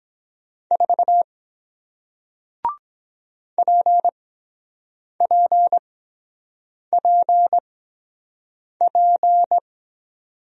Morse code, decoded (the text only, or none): 4 PPPP